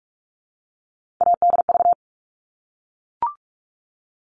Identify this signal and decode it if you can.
Morse code, heard as a keyed tone, about 44 wpm, 700 Hz. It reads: ADV
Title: All Cut Numbers - 3-Digit 45wpm